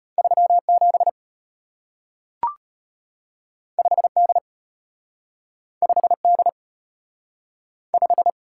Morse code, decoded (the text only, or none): 37 5D 5D 5